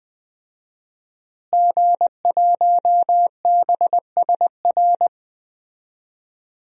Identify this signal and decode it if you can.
Morse code, heard as a keyed tone, about 20 wpm, 700 Hz. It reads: G1BSR
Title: DXCC-100 Call Signs - 5 Characters - Repeated 3x - 20wpm